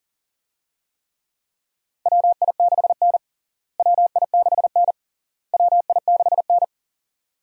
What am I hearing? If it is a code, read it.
Morse code, heard as a keyed tone, about 40 wpm, 700 Hz. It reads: WI6N WI6N WI6N
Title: USA Call Signs - 4 Characters - Repeated 3x - 40wpm